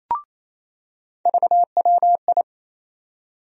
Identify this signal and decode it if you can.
Morse code, heard as a keyed tone, about 28 wpm, 700 Hz. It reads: VWI
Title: USA Call Sign Postfixes - 3 Letters 28wpm